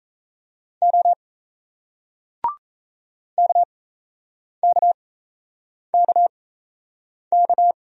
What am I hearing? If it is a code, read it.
Morse code, heard as a keyed tone, about 39 wpm, spaced as for 8 wpm, 700 Hz. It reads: O KKKK